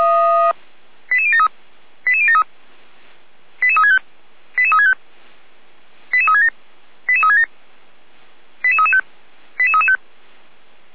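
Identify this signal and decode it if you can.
Five-tone selective call (ZVEI), heard as quick runs of five tones, tones of about 80 ms. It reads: 88073 88073 88365 88365 88367 88367 88335 88335
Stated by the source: ZVEI five-tone call received via SDR.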